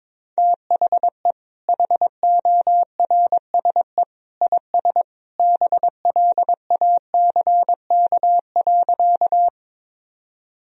Morse code, heard as a keyed tone, about 22 wpm, 700 Hz. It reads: THE HORSE IS BLACK.
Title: Sentences from Top 1000 Words - Review Part 2 - 22wpm